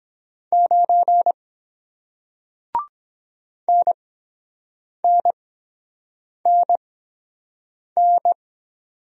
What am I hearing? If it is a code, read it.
Morse code, heard as a keyed tone, about 23 wpm, spaced as for 7 wpm, 700 Hz. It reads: 9 NNNN